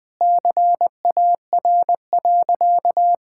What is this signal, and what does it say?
Morse code, heard as a keyed tone, about 21 wpm, 700 Hz. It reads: CAR.
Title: Sentences with 4-Word Phrases - 20wpm